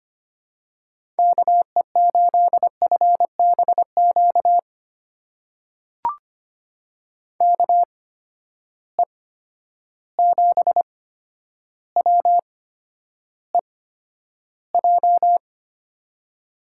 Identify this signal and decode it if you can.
Morse code, heard as a keyed tone, about 25 wpm, spaced as for 11 wpm, 700 Hz. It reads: KE8FBQ KE7WEJ
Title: USA Call Signs - 6 Characters - 8x Character Spacing - 25wpm